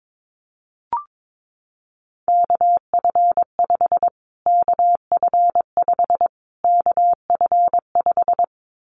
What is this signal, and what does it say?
Morse code, heard as a keyed tone, about 22 wpm, 700 Hz. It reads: KF5 KF5 KF5
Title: Callsign Prefixes for Top 25 DXCC - Repeated 3x - 22wpm